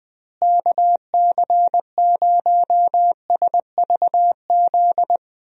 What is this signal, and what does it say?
Morse code, heard as a keyed tone, about 20 wpm, 700 Hz. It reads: KC0SVZ
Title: USA Call Signs - 6 Characters - Repeated 3x - 20wpm